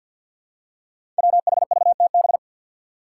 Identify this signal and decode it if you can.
Morse code, heard as a keyed tone, about 52 wpm, 700 Hz. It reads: WHVTB
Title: Five Challenging Letter-Number Combinations 50wpm